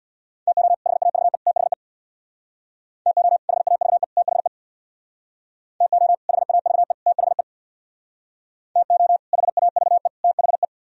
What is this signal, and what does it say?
Morse code, heard as a keyed tone, about 62 wpm, 700 Hz. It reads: TO HAVE THE TO HAVE THE TO HAVE THE TO HAVE THE
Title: Speed-Racing - Sets of 3 Words 50wpm